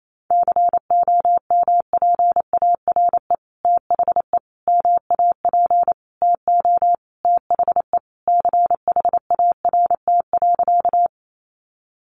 Morse code, heard as a keyed tone, about 28 wpm, 700 Hz. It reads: COMPARE THE MAP TO THE CHART.
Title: Sentences from Top 1300 Words 28wpm